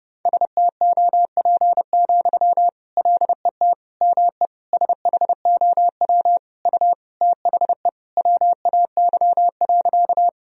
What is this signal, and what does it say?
Morse code, heard as a keyed tone, about 30 wpm, 700 Hz. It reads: STOP, LET ME SHOW U THE WAY.
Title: Sentences from Top 300 Words and Abbreviations 30wpm